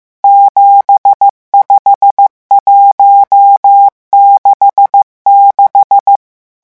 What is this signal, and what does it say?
Morse code, heard as a keyed tone, about 15 wpm, 800 Hz. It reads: 75166
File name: morse mp3 / revision11 chiffres vitesse 14